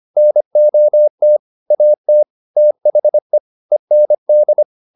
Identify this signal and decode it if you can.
Morse code, heard as a keyed tone, about 25 wpm, 600 Hz. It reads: NOT AT THE END